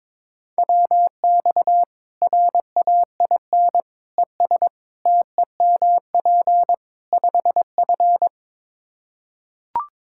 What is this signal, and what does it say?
Morse code, heard as a keyed tone, about 21 wpm, 700 Hz. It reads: WX RAIN ES TEMP 5F
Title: K1USN SST QSO Practice 22wpm